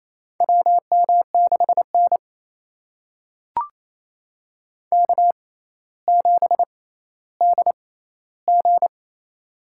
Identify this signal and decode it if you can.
Morse code, heard as a keyed tone, about 28 wpm, spaced as for 14 wpm, 700 Hz. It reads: WM6N K7DG
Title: USA Call Signs - 4 Characters - 6x Character Spacing - 28wpm